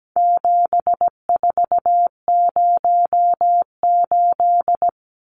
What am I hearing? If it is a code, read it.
Morse code, heard as a keyed tone, about 17 wpm, 700 Hz. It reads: 7408
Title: Numbers - 4-Digit 17wpm